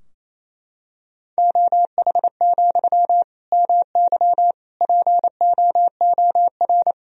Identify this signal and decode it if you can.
Morse code, heard as a keyed tone, about 28 wpm, 700 Hz. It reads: OH, MY POOR
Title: Sentences from Top 1000 Words - Review Part 3 - 28wpm